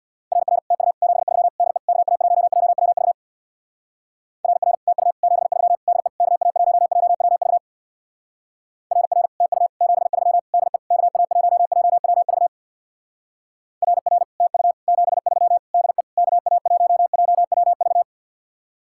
Morse code, heard as a keyed tone, about 62 wpm, 700 Hz. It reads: RR TU 73 DE KA1JWV RR TU 73 DE KA1JWV RR TU 73 DE KA1JWV RR TU 73 DE KA1JWV
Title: Speed-Racing Common QSO Phrases Encore - 50wpm